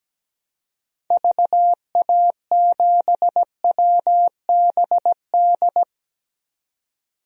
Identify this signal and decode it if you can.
Morse code, heard as a keyed tone, about 17 wpm, 700 Hz. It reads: VA7WBD
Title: DXCC-100 Call Signs - 6 Characters - 17wpm